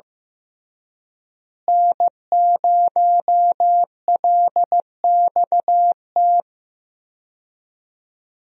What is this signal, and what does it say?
Morse code, heard as a keyed tone, about 15 wpm, 700 Hz. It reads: N0LXT
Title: USA Call Signs - 5 Characters - Repeated 3x - 15wpm